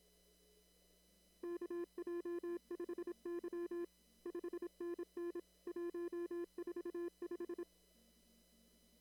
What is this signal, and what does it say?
Morse code, heard as a keyed tone, about 26 wpm, 340 Hz. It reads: KJ5Y 5NN 145
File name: CW / kt5j wpx cw-op-k5pi